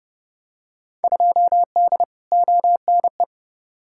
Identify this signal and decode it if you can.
Morse code, heard as a keyed tone, about 32 wpm, 700 Hz. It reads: 2D ONE
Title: Field Day Exchanges - 30wpm